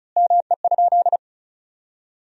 Morse code, heard as a keyed tone, about 34 wpm, 700 Hz. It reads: ME?
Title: 4x Word-Spacing - Sentences from Top 200 Words - 35wpm